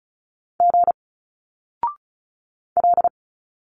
Morse code, heard as a keyed tone, about 35 wpm, 700 Hz. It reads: G L
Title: Single Letters 35wpm